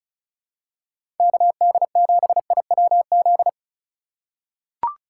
Morse code, heard as a keyed tone, about 34 wpm, 700 Hz. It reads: KD7IWZ
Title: USA Call Signs - 6 Characters - 2x Character Spacing - 35wpm